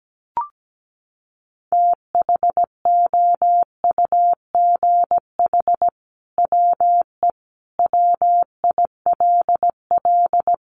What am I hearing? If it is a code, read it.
Morse code, heard as a keyed tone, about 17 wpm, 700 Hz. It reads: THOUGH WE WILL
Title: Sentences from Top 1000 Words 17wpm